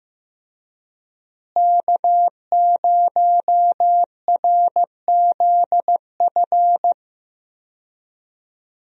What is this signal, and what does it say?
Morse code, heard as a keyed tone, about 15 wpm, 700 Hz. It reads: K0RZF
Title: USA Call Signs - 5 Characters - Repeated 2x - 15wpm